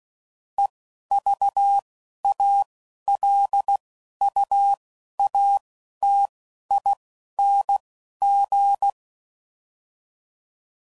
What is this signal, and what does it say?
Morse code, heard as a keyed tone, about 16 wpm, 800 Hz. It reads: E V A L U A T I N G
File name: Prefixes / By First Letter / Es